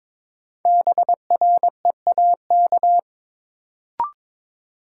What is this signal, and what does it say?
Morse code, heard as a keyed tone, about 22 wpm, 700 Hz. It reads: BREAK
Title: Sentences from Top 900 Words - New Word Drill - 22wpm